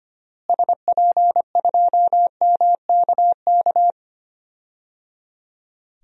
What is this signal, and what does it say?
Morse code, heard as a keyed tone, about 25 wpm, 700 Hz. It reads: SP2MKK
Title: DXCC-100 Call Signs - 6 Characters - Repeated 3x - 25wpm